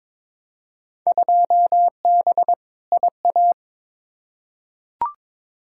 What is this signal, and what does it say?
Morse code, heard as a keyed tone, about 22 wpm, 700 Hz. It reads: 2B IA
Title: Field Day Exchanges Encore - 22wpm